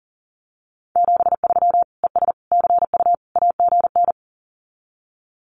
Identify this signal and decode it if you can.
Morse code, heard as a keyed tone, about 40 wpm, 700 Hz. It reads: 73 ES CU AGN
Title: Common QSO Phrases Encore 40wpm